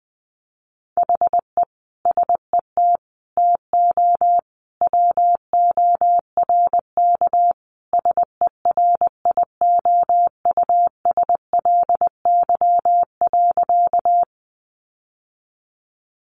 Morse code, heard as a keyed tone, about 20 wpm, 700 Hz. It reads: HE SET TO WORK SERIOUSLY.
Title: Sentences from Top 1200 Words 20wpm